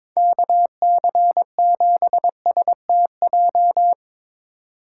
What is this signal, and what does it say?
Morse code, heard as a keyed tone, about 23 wpm, 700 Hz. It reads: KC7STJ
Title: USA Call Signs - 6 Characters - 4x Character Spacing - 22wpm